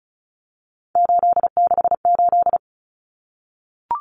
Morse code, heard as a keyed tone, about 34 wpm, 700 Hz. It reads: 868
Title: Numbers - 3-Digit 35wpm